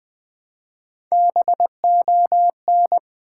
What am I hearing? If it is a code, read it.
Morse code, heard as a keyed tone, about 20 wpm, 700 Hz. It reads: BON
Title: Build A Word - 1 Letter At A Time - 4-Letter Words - 20wpm